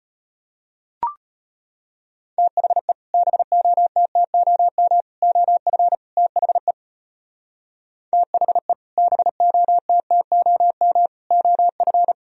Morse code, heard as a keyed tone, about 31 wpm, 700 Hz. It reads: THE BOTTOM OF THE THE BOTTOM OF
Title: Speed-Racing - Sets of 4 Words 25wpm